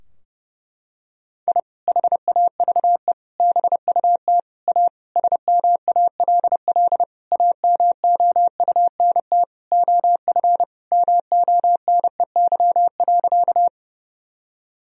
Morse code, heard as a keyed tone, about 30 wpm, 700 Hz. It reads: I HAVE BUT A SMALL AMOUNT OF MONEY.
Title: Sentences from Top 1000 Words - Review Part 4 - 30wpm